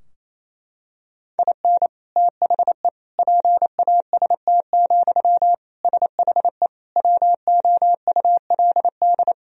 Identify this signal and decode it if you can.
Morse code, heard as a keyed tone, about 28 wpm, 700 Hz. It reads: IN THE PAST, SHE WOULD
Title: Sentences from Top 1000 Words - Review Part 3 - 28wpm